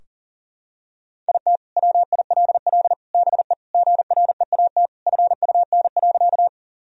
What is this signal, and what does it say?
Morse code, heard as a keyed tone, about 40 wpm, 700 Hz. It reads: IT WILL BE GREAT FUN.
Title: Sentences from Top 1000 Words - Review Part 4 - 40wpm